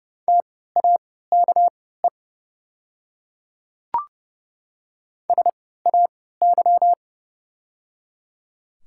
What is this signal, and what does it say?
Morse code, heard as a keyed tone, about 30 wpm, spaced as for 17 wpm, 700 Hz. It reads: TAKE SAY
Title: 3x Character Spacing - Top 500 Words - 30wpm